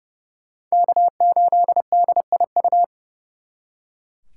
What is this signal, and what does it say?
Morse code, heard as a keyed tone, about 35 wpm, 700 Hz. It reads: K8DIU
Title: Speed Racing - USA 1x3 Call Signs 30wpm